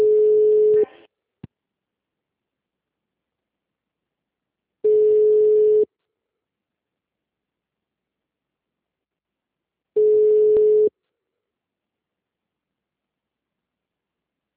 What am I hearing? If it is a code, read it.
Ringback tone (Europe, 425 Hz), with speech.